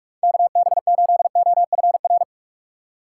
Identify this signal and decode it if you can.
Morse code, heard as a keyed tone, about 46 wpm, 700 Hz. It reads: KB8OFR
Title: USA Call Signs - 6x Character Spacing - 45wpm